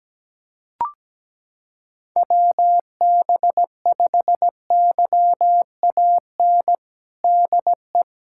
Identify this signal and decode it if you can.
Morse code, heard as a keyed tone, about 17 wpm, 700 Hz. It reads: WB5YAN DE
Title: USA Callsign DE Callsign 17wpm